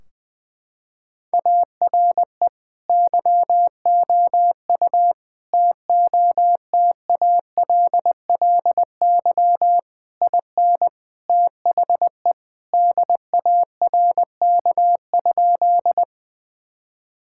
Morse code, heard as a keyed tone, about 21 wpm, 700 Hz. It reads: ARE YOU TOTALLY IN THE DARK?
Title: Sentences from Top 1600 Words 20wpm